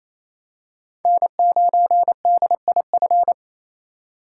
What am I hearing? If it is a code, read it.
Morse code, heard as a keyed tone, about 28 wpm, 700 Hz. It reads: N9DIF
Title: USA Call Signs - 5 Characters - 6x Character Spacing - 28wpm